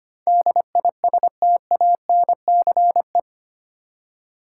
Morse code, heard as a keyed tone, about 25 wpm, 700 Hz. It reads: DISTANCE
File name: Eight Letter Words - 25wpm